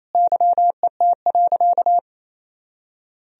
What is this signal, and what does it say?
Morse code, heard as a keyed tone, about 28 wpm, 700 Hz. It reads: YET.
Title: Sentences from Top 1000 Words - Review Part 1 - 28wpm